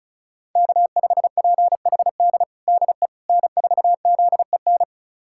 Morse code, heard as a keyed tone, about 34 wpm, 700 Hz. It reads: K5PHD DE N4ZEN
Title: USA Callsign DE Callsign - 5 Characters - 35wpm